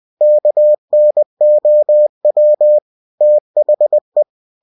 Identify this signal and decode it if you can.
Morse code, heard as a keyed tone, about 20 wpm, 600 Hz. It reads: KNOW THE
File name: cw-books / winnie-the-pooh / winnie-20wpm-page-0063